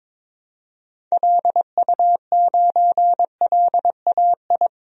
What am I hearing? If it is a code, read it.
Morse code, heard as a keyed tone, about 22 wpm, 700 Hz. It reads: LU9LAI
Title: DXCC-100 Call Signs - 6 Characters - 22wpm